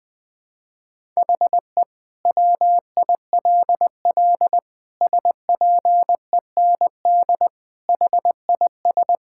Morse code, heard as a keyed tone, about 20 wpm, 700 Hz. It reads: HE WILL SPEND HIS
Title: Sentences from Top 1000 Words - Review Part 5 - 20wpm